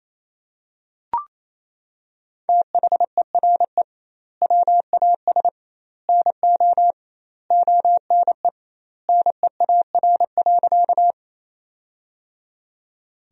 Morse code, heard as a keyed tone, about 27 wpm, 700 Hz. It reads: THERE WAS NO ONE NEAR.
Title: 2x Word-Spacing - Sentences from Top 200 Words - 28wpm